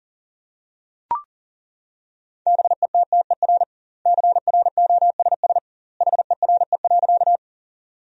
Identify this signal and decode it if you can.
Morse code, heard as a keyed tone, about 40 wpm, 700 Hz. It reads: BETTER CROSS HERE.
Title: 2x Word-Spacing - Sentences from Top 400 Words - 40wpm